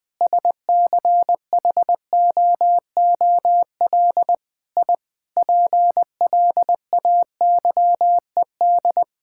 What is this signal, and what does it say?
Morse code, heard as a keyed tone, about 20 wpm, 700 Hz. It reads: SCHOOL I PLAYED
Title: Sentences with 6-Word Phrases - 20wpm